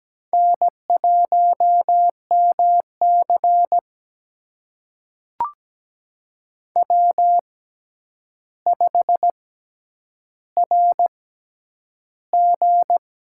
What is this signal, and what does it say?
Morse code, heard as a keyed tone, about 17 wpm, spaced as for 9 wpm, 700 Hz. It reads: N1MC W5RG
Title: USA Call Signs - 4 Characters - 6x Character Spacing - 17wpm